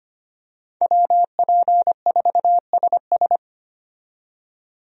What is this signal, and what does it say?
Morse code, heard as a keyed tone, about 25 wpm, 700 Hz. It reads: WP4SS
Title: USA Call Signs - 5 Characters - 25wpm